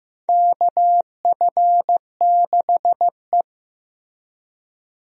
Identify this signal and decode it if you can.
Morse code, heard as a keyed tone, about 15 wpm, 700 Hz. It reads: KF6E
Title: USA Call Signs - 4 Characters - Repeated 2x - 15wpm